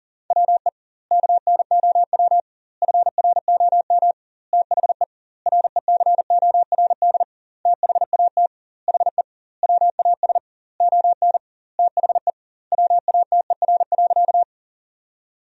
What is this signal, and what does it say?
Morse code, heard as a keyed tone, about 37 wpm, 700 Hz. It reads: WE KNOW FROM THE RECORD THAT HE WAS ON THE WATER.
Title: 2x Word-Spacing - Sentences from Top 500 Words - 40wpm